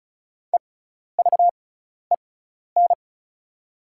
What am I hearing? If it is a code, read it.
Morse code, heard as a keyed tone, about 36 wpm, spaced as for 12 wpm, 700 Hz. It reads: EVEN
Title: 6x Character Spacing - Top 200 Words - 35wpm